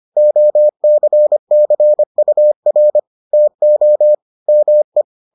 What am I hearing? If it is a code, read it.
Morse code, heard as a keyed tone, about 25 wpm, 600 Hz. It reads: OCCUR TO ME